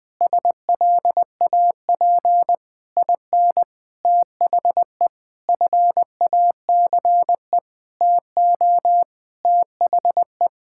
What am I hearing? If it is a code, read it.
Morse code, heard as a keyed tone, about 20 wpm, 700 Hz. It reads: SLAP IN THE FACE TO THE
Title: Sentences with 8-Word Phrases - 20wpm